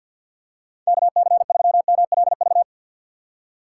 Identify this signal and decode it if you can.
Morse code, heard as a keyed tone, about 52 wpm, 700 Hz. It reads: KC3MLV
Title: USA Call Signs - 6 Characters - 4x Character Spacing - 50wpm